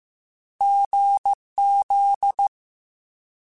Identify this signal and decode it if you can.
Morse code, heard as a keyed tone, about 15 wpm, 800 Hz. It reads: GZ